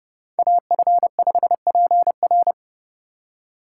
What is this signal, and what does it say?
Morse code, heard as a keyed tone, about 30 wpm, 700 Hz. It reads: AF5PR
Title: USA Call Signs - 5 Characters - 3x Character Spacing - 30wpm